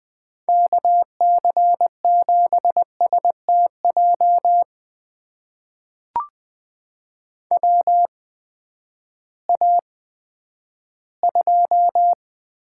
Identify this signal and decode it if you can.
Morse code, heard as a keyed tone, about 20 wpm, spaced as for 11 wpm, 700 Hz. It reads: KC7STJ WA2
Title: USA Call Signs - 6 Characters - 8x Character Spacing - 20wpm